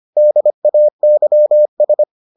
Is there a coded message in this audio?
Yes: Morse code from 0.2 to 2.0 s, heard as a keyed tone, about 25 wpm, 600 Hz. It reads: DAYS